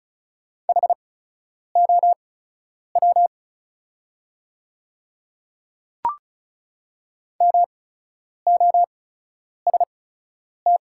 Morse code, heard as a keyed tone, about 36 wpm, spaced as for 10 wpm, 690 Hz. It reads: HOW MOST
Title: 8x Character Spacing - Top 200 Words - 35wpm